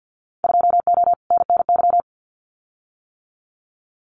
Morse code, heard as a keyed tone, about 50 wpm, 700 Hz. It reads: 2O NNY
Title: Winter Field Day Exchanges 50wpm